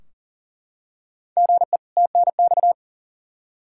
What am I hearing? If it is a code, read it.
Morse code, heard as a keyed tone, about 37 wpm, 700 Hz. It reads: GE TNX
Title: K1USN SST QSO Practice 40wpm